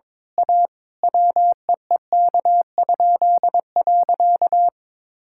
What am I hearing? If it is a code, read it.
Morse code, heard as a keyed tone, about 21 wpm, 700 Hz. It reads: A WEEK?.
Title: Pangram Challenge v2 - 22wpm